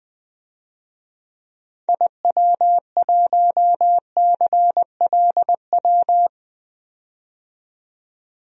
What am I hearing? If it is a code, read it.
Morse code, heard as a keyed tone, about 20 wpm, 700 Hz. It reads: IW1CLW
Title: DXCC-100 Call Signs - 6 Characters - 20wpm